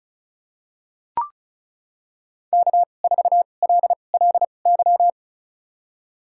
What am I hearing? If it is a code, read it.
Morse code, heard as a keyed tone, about 34 wpm, spaced as for 25 wpm, 700 Hz. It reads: K4LLY
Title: USA Call Signs - 5 Characters - 2x Character Spacing - 35wpm